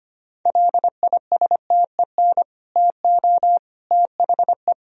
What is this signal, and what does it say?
Morse code, heard as a keyed tone, about 25 wpm, 700 Hz. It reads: LISTEN TO THE
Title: Sentences from Top 1000 Words - Review Part 3 - 25wpm